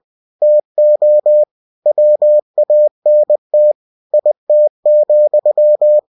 Morse code, heard as a keyed tone, about 20 wpm, 600 Hz. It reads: TO WANT IT,
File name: cw-books / winnie-the-pooh / winnie-20wpm-page-0032